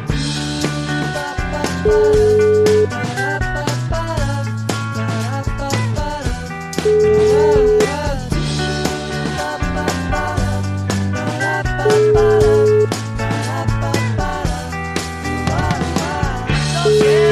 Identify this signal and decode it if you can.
Ringback tone (Europe, 425 Hz), with noise.